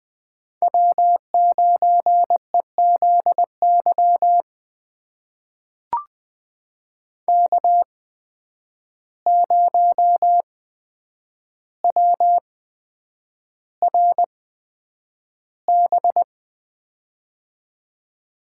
Morse code, heard as a keyed tone, about 20 wpm, spaced as for 9 wpm, 700 Hz. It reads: W9EZY K0WRB
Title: USA Call Signs - 5 Characters - 8x Character Spacing - 20wpm